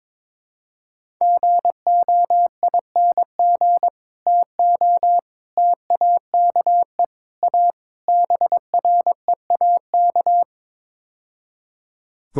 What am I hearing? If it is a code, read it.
Morse code, heard as a keyed tone, about 22 wpm, 700 Hz. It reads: GOING TO TAKE A BREAK
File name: practice / 22wpm / Build A Phrase - 5-Word Phrases - 22wpm